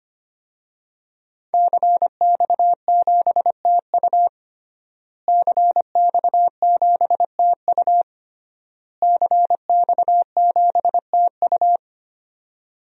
Morse code, heard as a keyed tone, about 25 wpm, 700 Hz. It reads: CX7TU CX7TU CX7TU
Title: DXCC-100 Call Signs - 5 Characters - Repeated 3x - 25wpm